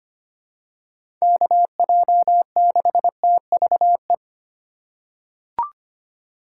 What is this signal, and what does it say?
Morse code, heard as a keyed tone, about 25 wpm, 700 Hz. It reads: KJ6TVE
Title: USA Call Signs - 6 Characters - 6x Character Spacing - 25wpm